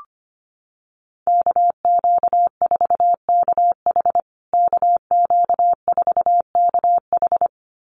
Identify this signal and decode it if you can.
Morse code, heard as a keyed tone, about 25 wpm, 700 Hz. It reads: KQ4KH KQ4KH
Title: USA Call Signs Repeated 2x 25wpm